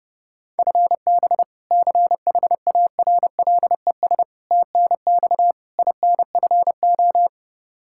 Morse code, heard as a keyed tone, about 30 wpm, 700 Hz. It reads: FB CHARLES TNX INFO
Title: Practice QSO - Exchange 3 - More Info - 30wpm